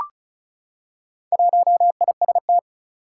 Morse code, heard as a keyed tone, about 36 wpm, 700 Hz. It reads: 1IST
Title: Four Challenging Letter-Number Combinations 35wpm